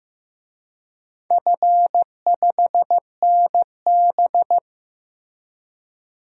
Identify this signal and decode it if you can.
Morse code, heard as a keyed tone, about 15 wpm, 700 Hz. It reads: F5NB
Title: DXCC-100 Call Signs - 4 Characters - Repeated 2x - 15wpm